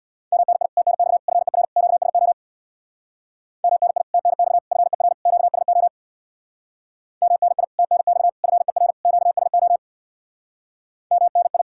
Morse code, heard as a keyed tone, about 60 wpm, 700 Hz. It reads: MNI TNX FER QSO MNI TNX FER QSO MNI TNX FER QSO MNI
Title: Speed-Racing Common QSO Phrases - 50wpm